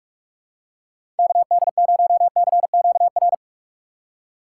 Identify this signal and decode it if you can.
Morse code, heard as a keyed tone, about 44 wpm, 700 Hz. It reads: KD0CQR
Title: USA Call Signs - 6 Characters - 6x Character Spacing - 45wpm